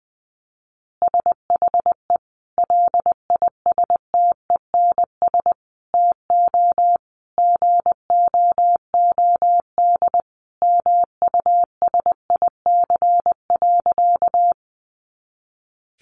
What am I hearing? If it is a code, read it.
Morse code, heard as a keyed tone, about 20 wpm, 700 Hz. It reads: SHE LISTENS TO GOOD MUSIC.
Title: Easy Phrases 20wpm